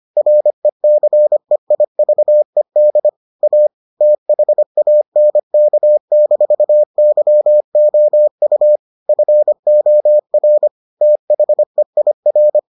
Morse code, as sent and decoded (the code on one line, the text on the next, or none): .-. -.-. .. ...- . -.. / .- / - .... .- -. -.- -....- -.-- --- ..- / ..-. --- .-. / - .... .. .-.
RECEIVED A THANK-YOU FOR THEIR